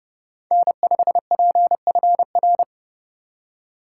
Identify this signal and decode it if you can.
Morse code, heard as a keyed tone, about 30 wpm, 700 Hz. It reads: N5PFR
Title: USA Call Signs - 5 Characters - 6x Character Spacing - 30wpm